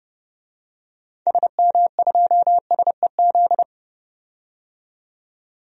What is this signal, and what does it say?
Morse code, heard as a keyed tone, about 31 wpm, 700 Hz. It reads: SM2SEZ
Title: DXCC-100 Call Signs - 6 Characters - 30wpm